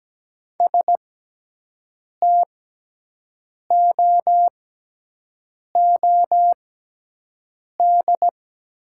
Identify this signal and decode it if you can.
Morse code, heard as a keyed tone, about 17 wpm, spaced as for 6 wpm, 700 Hz. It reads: STOOD
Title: 6x Character Spacing - Top 500 Words - 17wpm